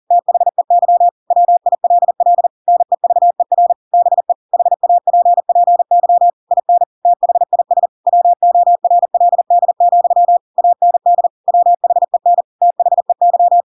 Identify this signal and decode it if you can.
Morse code, heard as a keyed tone, about 40 wpm, 700 Hz. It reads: THEY WILL NEVER BE HAPPY IN THIS WORLD, AND WHEN THEY
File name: Pinocchio / Pinocchio ch4 40 40 ik1hge